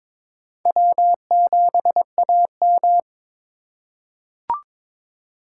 Morse code, heard as a keyed tone, about 22 wpm, 700 Hz. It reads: W7AM
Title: USA Call Signs - 4 Characters - 4x Character Spacing - 22wpm